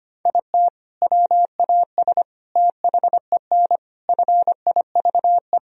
Morse code, heard as a keyed tone, about 25 wpm, 700 Hz. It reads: IT WAS THEN FIVE
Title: Sentences from Top 1000 Words - Review Part 2 - 25wpm